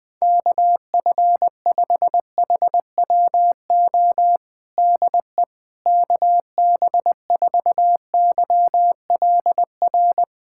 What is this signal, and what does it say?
Morse code, heard as a keyed tone, about 20 wpm, 700 Hz. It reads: KF5HWO DE KB4YLR
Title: USA Callsign DE Callsign - 6 Characters - 20wpm